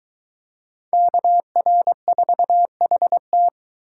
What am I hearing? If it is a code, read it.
Morse code, heard as a keyed tone, about 23 wpm, 700 Hz. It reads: KR4HT